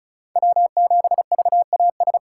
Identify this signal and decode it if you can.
Morse code, heard as a keyed tone, about 36 wpm, 700 Hz. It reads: W7VAS
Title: K1USN SST QSO Practice - 35wpm